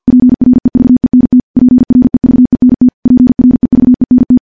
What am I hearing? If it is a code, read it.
Morse code, heard as a keyed tone, about 46 wpm, 250 Hz. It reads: PREVENT PREVENT PREVENT
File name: morse / morse-training-50WPM